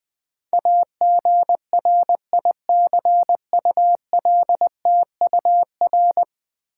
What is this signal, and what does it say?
Morse code, heard as a keyed tone, about 20 wpm, 700 Hz. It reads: AGRICULTUR
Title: Build A Word - 5 Letters At A Time - 12-Letter Words - 20wpm